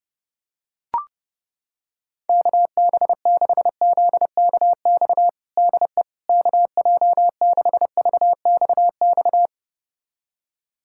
Morse code, heard as a keyed tone, about 32 wpm, 700 Hz. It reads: KB6ZKX DE KJ6VXX
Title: USA Callsign DE Callsign - 6 Characters - 30wpm